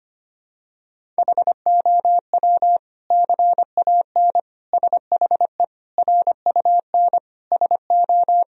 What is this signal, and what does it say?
Morse code, heard as a keyed tone, about 25 wpm, 700 Hz. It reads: HOW CAN SHE RUN SO
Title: Sentences from Top 1000 Words - Review Part 2 - 25wpm